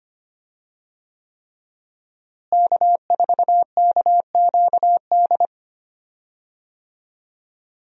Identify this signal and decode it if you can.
Morse code, heard as a keyed tone, about 25 wpm, 700 Hz. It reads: K4KQD
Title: USA Call Signs - 5 Characters - Repeated 3x - 25wpm